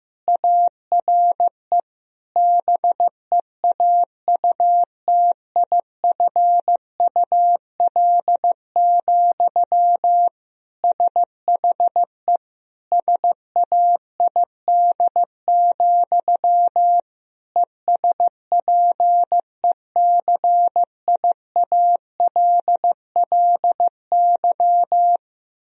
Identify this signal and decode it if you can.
Morse code, heard as a keyed tone, about 15 wpm, 700 Hz. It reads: ARE BEAUTIFUL, SHE SAID, ESPECIALLY
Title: Sentences from Top 1000 Words - Review Part 4 - 15wpm